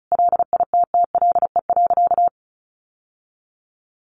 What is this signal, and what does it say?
Morse code, heard as a keyed tone, about 34 wpm, 700 Hz. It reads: LITTLE.
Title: Sentences from Top 800 Words 35wpm